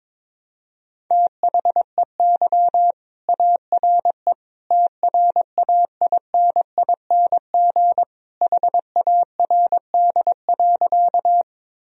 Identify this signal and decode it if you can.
Morse code, heard as a keyed tone, about 22 wpm, 700 Hz. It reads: THEY ARE TRAINING HARD.
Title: Sentences from Top 1000 Words - Review Part 4 - 22wpm